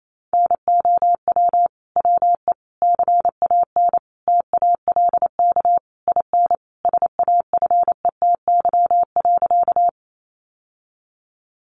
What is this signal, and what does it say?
Morse code, heard as a keyed tone, about 28 wpm, 700 Hz. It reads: NOW WE CAN TALK IN SAFETY.
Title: Sentences from Top 700 Words 28wpm